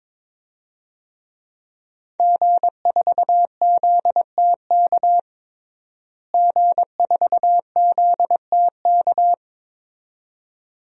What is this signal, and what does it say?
Morse code, heard as a keyed tone, about 21 wpm, 700 Hz. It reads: G4ZTK G4ZTK
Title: DXCC-100 Call Signs - 5 Characters - Repeated 2x - 22wpm